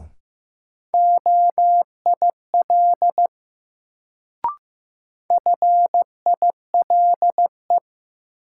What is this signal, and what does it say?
Morse code, heard as a keyed tone, about 15 wpm, 700 Hz. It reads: OIL FILE
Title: Sentences from Top 1000 Words - Review Part 4 - New Word Drill - 15wpm